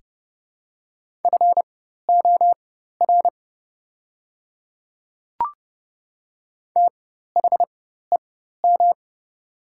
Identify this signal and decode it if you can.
Morse code, heard as a keyed tone, about 30 wpm, spaced as for 14 wpm, 700 Hz. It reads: FOR THEM
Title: 4x Character Spacing - Top 100 Words - 30wpm